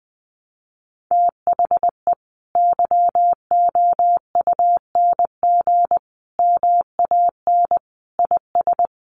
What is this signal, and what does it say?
Morse code, heard as a keyed tone, about 20 wpm, 700 Hz. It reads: THE YOUNG MAN IS
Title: Sentences from Top 1300 Words 20wpm